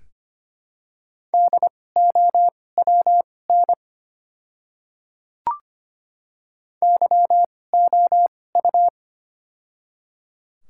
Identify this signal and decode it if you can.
Morse code, heard as a keyed tone, about 25 wpm, spaced as for 18 wpm, 700 Hz. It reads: DOWN YOU
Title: 2x Character Spacing - Top 100 Words - 25wpm